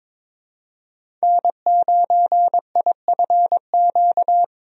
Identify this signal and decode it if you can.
Morse code, heard as a keyed tone, about 22 wpm, 700 Hz. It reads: N9IFQ
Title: USA Call Signs - 5 Characters - 22wpm